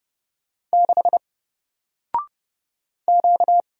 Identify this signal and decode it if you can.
Morse code, heard as a keyed tone, about 31 wpm, 700 Hz. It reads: 6 Q
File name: Single-Letter-Number-30wpm